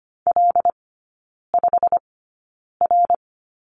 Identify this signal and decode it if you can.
Morse code, heard as a keyed tone, about 25 wpm, spaced as for 9 wpm, 690 Hz. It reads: L5R
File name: Lesson 15e - Number 5 - Three Characters 25wpm